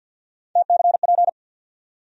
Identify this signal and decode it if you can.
Morse code, heard as a keyed tone, about 53 wpm, 700 Hz. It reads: TCP
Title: USA Call Sign Postfixes - 3 Letters 50wpm